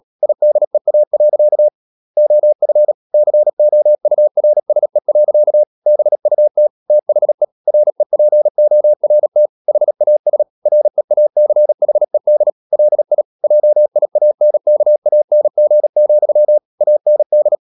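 Morse code, heard as a keyed tone, about 37 wpm, 600 Hz. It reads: IDEA. OF COURSE. BUT THE REPORT HAS REACHED LI JIANKANG, AND
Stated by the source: Morse practice at 36 wpm, text of Ken Follett's novel Never, from CW med Gnister.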